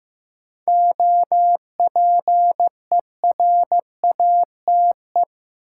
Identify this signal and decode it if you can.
Morse code, heard as a keyed tone, about 15 wpm, 700 Hz. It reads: OPERATE
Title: Sentences from Top 1500 Words - New Word Drill - 15wpm